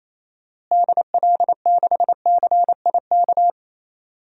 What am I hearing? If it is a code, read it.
Morse code, heard as a keyed tone, about 28 wpm, 700 Hz. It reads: DL6CIK
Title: DXCC-100 Call Signs - 6 Characters - Repeated 3x - 28wpm